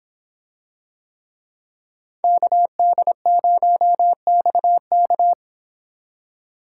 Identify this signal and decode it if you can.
Morse code, heard as a keyed tone, about 26 wpm, 700 Hz. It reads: KD0XK